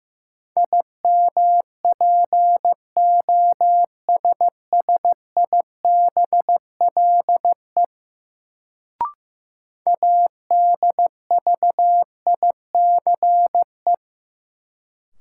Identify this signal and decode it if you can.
Morse code, heard as a keyed tone, about 15 wpm, 700 Hz. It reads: IMPOSSIBLE ADVICE
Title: Sentences from Top 1100 Words - New Word Drill Alt - 15wpm